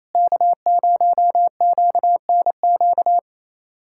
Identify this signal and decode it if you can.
Morse code, heard as a keyed tone, about 28 wpm, 700 Hz. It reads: K0QNQ
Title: USA Callsign DE Callsign - 5 Characters - 28wpm